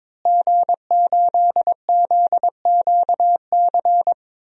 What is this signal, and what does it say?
Morse code, heard as a keyed tone, about 22 wpm, 700 Hz. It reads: G8ZQC
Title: DXCC-100 Call Signs - 5 Characters - Repeated 3x - 22wpm